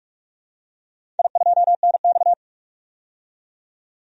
Morse code, heard as a keyed tone, about 46 wpm, 700 Hz. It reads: I2NX
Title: DXCC-100 Call Signs - 4 Characters - 45wpm